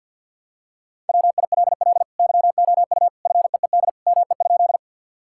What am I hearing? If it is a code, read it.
Morse code, heard as a keyed tone, about 50 wpm, 700 Hz. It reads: WILL YOU FEED ME?
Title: Sentences from Top 1600 Words 50wpm